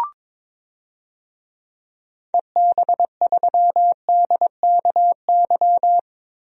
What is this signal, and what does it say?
Morse code, heard as a keyed tone, about 22 wpm, 700 Hz. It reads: EB3DKY
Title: DXCC-100 Call Signs - 6 Characters - 22wpm